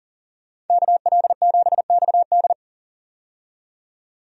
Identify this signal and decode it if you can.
Morse code, heard as a keyed tone, about 40 wpm, 700 Hz. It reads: KL7XD
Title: DXCC-100 Call Signs - 5 Characters - 40wpm